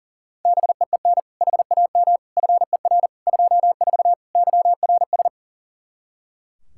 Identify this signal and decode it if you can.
Morse code, heard as a keyed tone, about 40 wpm, 700 Hz. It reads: BEEN HAM FER 24 YRS
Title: Practice QSO - Exchange 3 - More Info - 40wpm